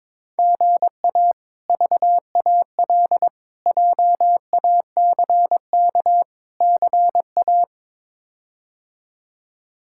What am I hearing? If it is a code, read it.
Morse code, heard as a keyed tone, about 22 wpm, 700 Hz. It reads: GA VAL JACK CA
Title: K1USN SST QSO Practice - 22wpm